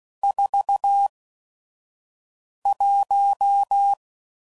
Morse code, heard as a keyed tone, about 16 wpm, spaced as for 5 wpm, 800 Hz. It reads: 41
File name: CW Code Groups @5WPM / Weather Sets / Weather 4 14